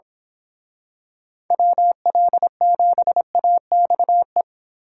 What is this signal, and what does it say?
Morse code, heard as a keyed tone, about 26 wpm, 700 Hz. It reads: WL7AXE
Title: Speed Racing - USA 2x3 Call Signs 22wpm